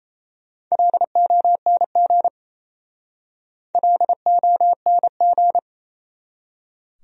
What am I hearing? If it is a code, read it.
Morse code, heard as a keyed tone, about 37 wpm, 700 Hz. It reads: LONG LONG